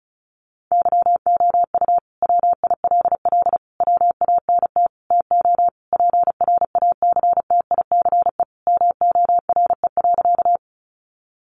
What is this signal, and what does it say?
Morse code, heard as a keyed tone, about 34 wpm, 700 Hz. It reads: YOU WILL WANT TO PRACTICE MORE.
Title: Sentences from Top 700 Words 35wpm